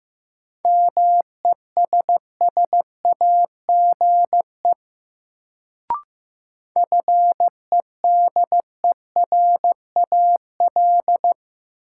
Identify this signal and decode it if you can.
Morse code, heard as a keyed tone, about 15 wpm, 700 Hz. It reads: MESSAGE FEDERAL
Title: Sentences from Top 1100 Words - New Word Drill Alt - 15wpm